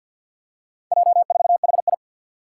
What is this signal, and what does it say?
Morse code, heard as a keyed tone, about 52 wpm, 700 Hz. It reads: J4HI
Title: Four Challenging Letter-Number Combinations 50wpm